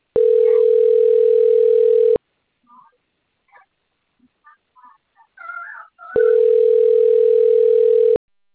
Ringback tone (North America), 440 + 480 Hz.